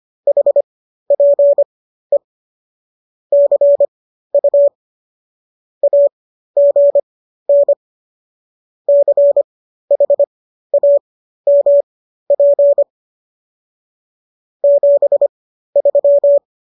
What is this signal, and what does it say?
Morse code, heard as a keyed tone, about 25 wpm, spaced as for 13 wpm, 570 Hz. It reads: HPE CU AGN CHAMP 73